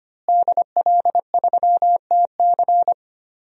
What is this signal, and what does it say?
Morse code, heard as a keyed tone, about 25 wpm, 700 Hz. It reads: DL3TC
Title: DXCC-100 Call Signs - 5 Characters - Repeated 3x - 25wpm